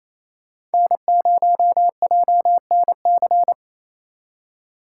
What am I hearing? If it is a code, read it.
Morse code, heard as a keyed tone, about 36 wpm, 700 Hz. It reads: N0JNC